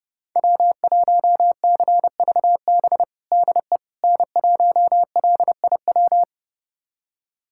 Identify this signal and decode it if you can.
Morse code, heard as a keyed tone, about 30 wpm, 700 Hz. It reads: W1CVB DE N1LIW
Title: USA Callsign DE Callsign - 5 Characters - 30wpm